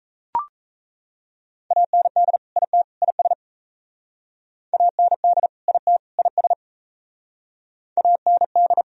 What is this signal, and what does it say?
Morse code, heard as a keyed tone, about 35 wpm, 700 Hz. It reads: AND IT IS AND IT IS AND
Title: Speed-Racing - Sets of 3 Words 28wpm